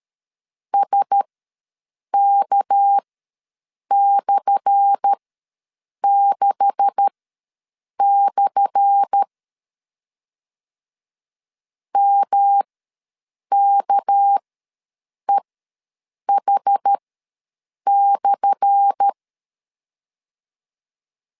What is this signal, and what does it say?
Morse code, heard as a keyed tone, about 13 wpm, spaced as for 7 wpm, 800 Hz. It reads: SK/6/ MKEH/
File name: LEZIONI / 10 SIGNS / 10A SLASH ws 12wpm 3443